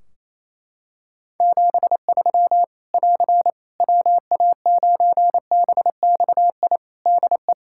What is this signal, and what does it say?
Morse code, heard as a keyed tone, about 28 wpm, 700 Hz. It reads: 73 <AR> WA9BXI DE
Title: Practice QSO - Exchange 4 - Ending - 28wpm